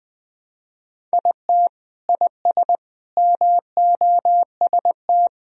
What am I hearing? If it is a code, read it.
Morse code, heard as a keyed tone, about 20 wpm, 700 Hz. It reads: IT IS MOST
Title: Sentences from Top 1000 Words - Review Part 1 - 20wpm